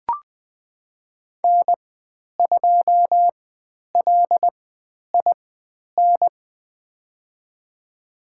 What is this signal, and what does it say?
Morse code, heard as a keyed tone, about 20 wpm, spaced as for 10 wpm, 700 Hz. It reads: N2LIN
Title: 095 Lesson 16g - Letter c - Call Signs 20wpm